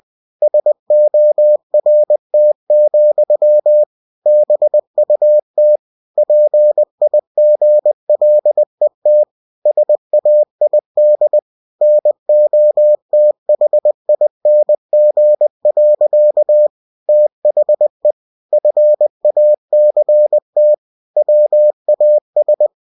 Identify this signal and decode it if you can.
Morse code, heard as a keyed tone, about 20 wpm, 600 Hz. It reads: SORT, BUT PIGLET SAID NOTHING. THE FACT WAS
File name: cw-books / winnie-the-pooh / winnie-20wpm-page-0034